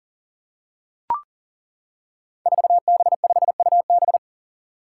Morse code, heard as a keyed tone, about 40 wpm, 700 Hz. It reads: 4B5UB
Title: Five Challenging Letter-Number Combinations 40wpm